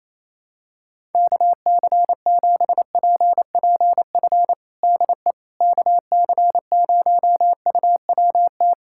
Morse code, heard as a keyed tone, about 27 wpm, 700 Hz. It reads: KC7PPF DE KC0UWT
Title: USA Callsign DE Callsign - 6 Characters - 28wpm